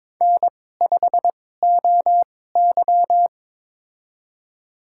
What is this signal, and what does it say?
Morse code, heard as a keyed tone, about 21 wpm, spaced as for 16 wpm, 700 Hz. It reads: N5OY
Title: USA Call Signs - 2x Character Spacing - 22wpm